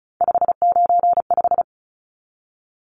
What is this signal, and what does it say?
Morse code, heard as a keyed tone, about 34 wpm, 700 Hz. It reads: 595
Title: Numbers - 3-Digit 35wpm